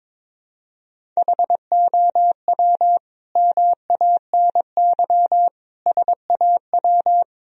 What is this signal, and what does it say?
Morse code, heard as a keyed tone, about 21 wpm, 700 Hz. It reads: HOW MANY SAW
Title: Sentences from Top 1000 Words - Review Part 1 - 22wpm